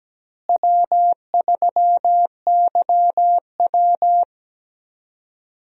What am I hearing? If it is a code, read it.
Morse code, heard as a keyed tone, about 17 wpm, 700 Hz. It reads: W3YW
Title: USA Call Signs - 4 Characters - Repeated 3x - 17wpm